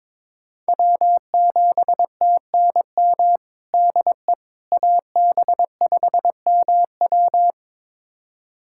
Morse code, heard as a keyed tone, about 22 wpm, 700 Hz. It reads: W7TNM DE AB5MW
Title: USA Callsign DE Callsign - 5 Characters - 22wpm